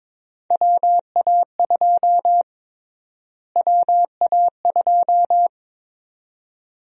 Morse code, heard as a keyed tone, about 22 wpm, 700 Hz. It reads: WA2 WA2
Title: USA Call Sign Prefixes - Repeated 2x - 22wpm